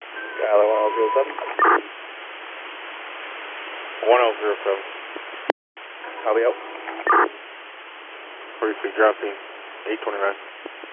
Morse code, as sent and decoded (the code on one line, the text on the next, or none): none